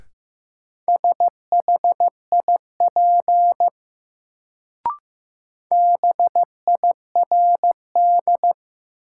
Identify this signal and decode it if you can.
Morse code, heard as a keyed tone, about 15 wpm, 700 Hz. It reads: SHIP BIRD
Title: Sentences from Top 400 Words - New Word Drill - 15wpm